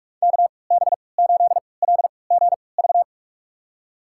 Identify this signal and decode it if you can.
Morse code, heard as a keyed tone, about 46 wpm, spaced as for 26 wpm, 700 Hz. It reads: KB8LGV
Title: USA Call Signs - 6 Characters - 3x Character Spacing - 45wpm